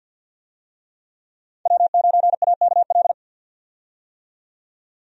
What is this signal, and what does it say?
Morse code, heard as a keyed tone, about 50 wpm, 700 Hz. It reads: W9AKL
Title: USA Call Signs - 5 Characters - 50wpm